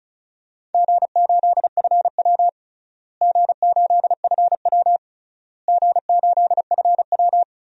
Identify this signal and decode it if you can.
Morse code, heard as a keyed tone, about 36 wpm, 700 Hz. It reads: G8FW G8FW G8FW
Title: DXCC-100 Call Signs - 4 Characters - Repeated 3x - 35wpm